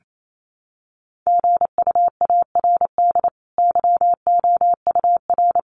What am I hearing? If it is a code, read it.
Morse code, heard as a keyed tone, about 28 wpm, 700 Hz. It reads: GUARD YOUR
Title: Sentences from Top 1200 Words 28wpm